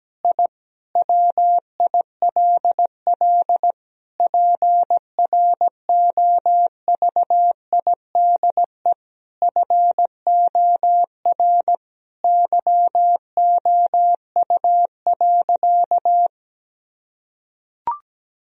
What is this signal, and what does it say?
Morse code, heard as a keyed tone, about 17 wpm, 700 Hz. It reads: I WILL PROVIDE FOR YOU.
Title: Sentences from Top 1000 Words - Review Part 3 - 17wpm